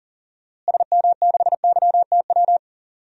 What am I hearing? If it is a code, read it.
Morse code, heard as a keyed tone, about 40 wpm, 700 Hz. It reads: SM6YTW
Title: DXCC-100 Call Signs - 6 Characters - Repeated 2x - 40wpm